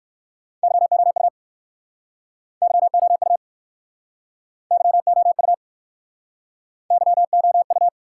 Morse code, heard as a keyed tone, about 62 wpm, 700 Hz. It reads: YOU YOU YOU YOU